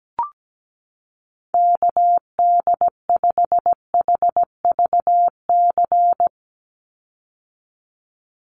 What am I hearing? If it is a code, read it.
Morse code, heard as a keyed tone, about 17 wpm, 700 Hz. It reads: KD5HVC
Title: USA Call Signs 17wpm